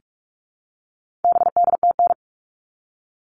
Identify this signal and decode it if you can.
Morse code, heard as a keyed tone, about 44 wpm, 700 Hz. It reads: BDTN
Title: All Cut Numbers - 4-Digit 45wpm